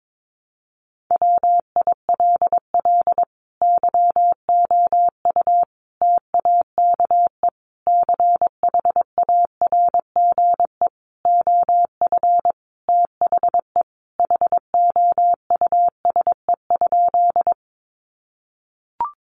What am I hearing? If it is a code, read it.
Morse code, heard as a keyed tone, about 22 wpm, 700 Hz. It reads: WILL YOU TAKE CHARGE OF THE HOUSE?
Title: Sentences from Top 900 Words 22wpm